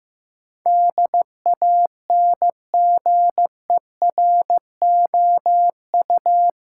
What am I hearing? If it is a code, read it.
Morse code, heard as a keyed tone, about 15 wpm, 700 Hz. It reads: DANGEROU
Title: Sentences from Top 1200 Words - New Word Drill - 15wpm